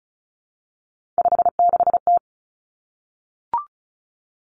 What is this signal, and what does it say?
Morse code, heard as a keyed tone, about 36 wpm, 700 Hz. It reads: 56T
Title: Three Challenging Letter-Number Combinations 35wpm